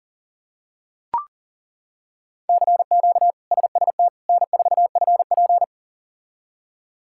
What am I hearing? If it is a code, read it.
Morse code, heard as a keyed tone, about 40 wpm, 700 Hz. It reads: CQ SST N4FP
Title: K1USN SST QSO Practice - 40wpm